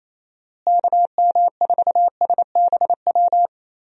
Morse code, heard as a keyed tone, about 28 wpm, 700 Hz. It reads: KM4SBW
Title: USA Call Signs - 6 Characters - Repeated 3x - 28wpm